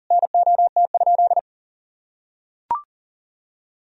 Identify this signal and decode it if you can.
Morse code, heard as a keyed tone, about 40 wpm, 700 Hz. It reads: NOT?
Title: Sentences from Top 1000 Words - Review Part 1 - 40wpm